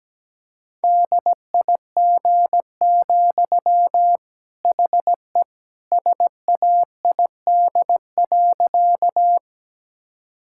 Morse code, heard as a keyed tone, about 17 wpm, 700 Hz. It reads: DIG, HE SAID.
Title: Sentences from Top 1600 Words 17wpm